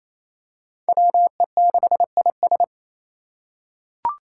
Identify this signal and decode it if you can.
Morse code, heard as a keyed tone, about 27 wpm, 700 Hz. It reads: WE6IS
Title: Five Challenging Letter-Number Combinations 28wpm